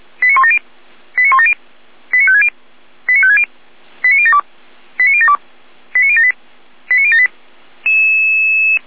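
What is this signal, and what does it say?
Five-tone selective call (ZVEI), heard as quick runs of five tones, tones of about 70 ms. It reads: 89170 89170 89560 89560 89072 89072 89078 89078
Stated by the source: ZVEI five-tone call 89150 received by SDR.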